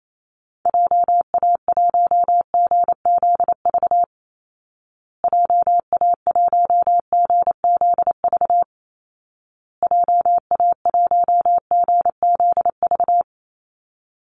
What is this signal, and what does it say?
Morse code, heard as a keyed tone, about 28 wpm, 700 Hz. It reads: JA1GZV JA1GZV JA1GZV
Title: Call Signs Repeated 2x - 28wpm